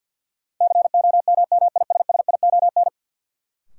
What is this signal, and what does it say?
Morse code, heard as a keyed tone, about 48 wpm, 700 Hz. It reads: COMMISSION
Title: Sentences from Top 1400 Words - New Word Drill - 50wpm